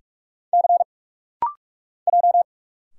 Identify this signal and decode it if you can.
Morse code, heard as a keyed tone, about 45 wpm, 700 Hz. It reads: C J
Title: Single Letter-Number - Rapid-Fire 45wpm